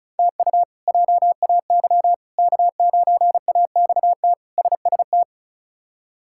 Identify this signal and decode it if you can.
Morse code, heard as a keyed tone, about 36 wpm, 700 Hz. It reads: TU JAY K9AXT SST
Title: K1USN SST QSO Practice - 35wpm